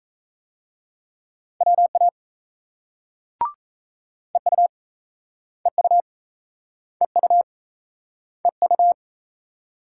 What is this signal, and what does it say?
Morse code, heard as a keyed tone, about 38 wpm, 700 Hz. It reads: WA EU EU EU EU